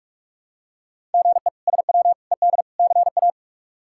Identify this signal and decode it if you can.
Morse code, heard as a keyed tone, about 42 wpm, 700 Hz. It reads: GE SW ED CA
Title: K1USN SST QSO Practice - 45wpm